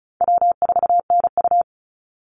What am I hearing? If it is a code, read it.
Morse code, heard as a keyed tone, about 34 wpm, 700 Hz. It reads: W4NU
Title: Call Signs - 35wpm